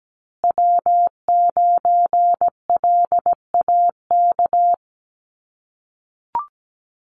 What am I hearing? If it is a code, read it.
Morse code, heard as a keyed tone, about 17 wpm, 700 Hz. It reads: W9LAK
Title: USA Call Signs Repeated 3x 17wpm